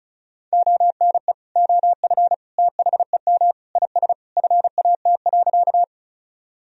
Morse code, heard as a keyed tone, about 34 wpm, 700 Hz. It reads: ONE OF THEM IS FAT.
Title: Sentences from Top 1000 Words - Review Part 4 - 35wpm